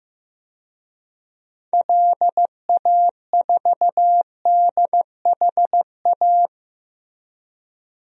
Morse code, heard as a keyed tone, about 16 wpm, 700 Hz. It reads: LA4DHA
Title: DXCC-100 Call Signs - 6 Characters - Repeated 3x - 15wpm